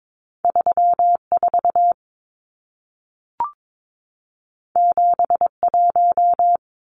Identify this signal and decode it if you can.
Morse code, heard as a keyed tone, about 22 wpm, 700 Hz. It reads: 34 71
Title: Numbers - 2-Digit 22wpm